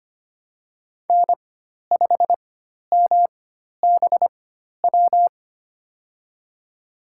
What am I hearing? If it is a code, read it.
Morse code, heard as a keyed tone, about 25 wpm, spaced as for 12 wpm, 700 Hz. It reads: N5MBW
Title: USA Call Signs - 5 Characters - 4x Character Spacing - 25wpm